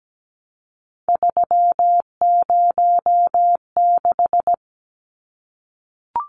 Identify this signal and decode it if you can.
Morse code, heard as a keyed tone, about 17 wpm, 700 Hz. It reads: 306
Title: Numbers - 3-Digit 17wpm